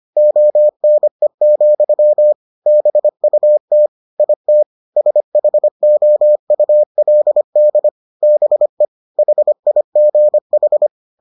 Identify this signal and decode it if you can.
Morse code, heard as a keyed tone, about 25 wpm, 600 Hz. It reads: ONE, BUT IT SHOULD BE HIGH